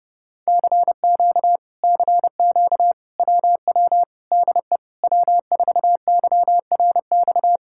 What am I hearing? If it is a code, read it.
Morse code, heard as a keyed tone, about 29 wpm, 700 Hz. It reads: CQ CQ WW DE W4YRX